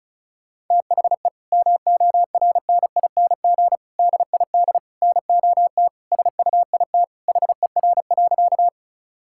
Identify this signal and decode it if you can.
Morse code, heard as a keyed tone, about 34 wpm, 700 Hz. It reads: THE MORNING DID NOT SUIT HER.
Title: Sentences from Top 1600 Words 35wpm